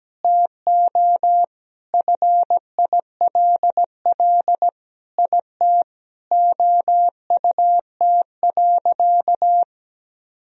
Morse code, heard as a keyed tone, about 17 wpm, 700 Hz. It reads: TO FILL IT OUT.
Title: Sentences from Top 1000 Words - Review Part 3 - 17wpm